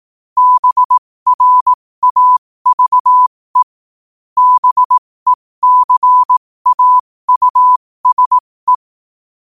Morse code, heard as a keyed tone, about 18 wpm, spaced as for 15 wpm, 1000 Hz. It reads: BRAVE BECAUSE
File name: qotdfiles / 15wpm